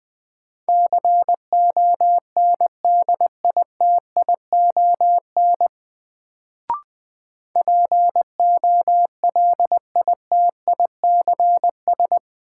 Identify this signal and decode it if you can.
Morse code, heard as a keyed tone, about 20 wpm, 700 Hz. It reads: CONDITION POLITICS
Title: Sentences from Top 1000 Words - New Word Drill - 20wpm